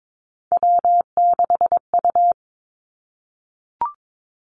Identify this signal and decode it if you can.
Morse code, heard as a keyed tone, about 22 wpm, 700 Hz. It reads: W6U
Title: Three Challenging Letter-Number Combinations 22wpm